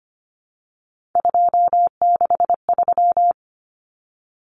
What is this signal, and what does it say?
Morse code, heard as a keyed tone, about 25 wpm, 700 Hz. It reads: 263
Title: Numbers - 3-Digit 25wpm